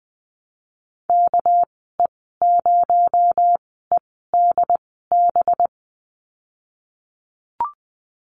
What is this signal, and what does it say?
Morse code, heard as a keyed tone, about 20 wpm, spaced as for 15 wpm, 700 Hz. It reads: KE0EDB
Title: 238 Lesson 38g - Letter q - Call Signs 20wpm